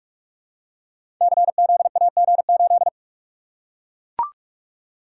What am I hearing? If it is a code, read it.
Morse code, heard as a keyed tone, about 43 wpm, 700 Hz. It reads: CZAG8
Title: Code Groups - 5 Letter:Numbers - 45wpm